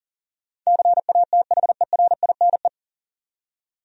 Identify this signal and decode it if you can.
Morse code, heard as a keyed tone, about 40 wpm, 700 Hz. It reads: CATHERINE
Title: USA Common Names - 40wpm